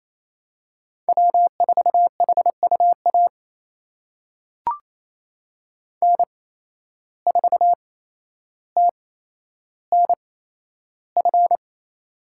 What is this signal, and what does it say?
Morse code, heard as a keyed tone, about 28 wpm, spaced as for 12 wpm, 700 Hz. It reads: W4HUA N4TNF
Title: USA Call Signs - 5 Characters - 8x Character Spacing - 28wpm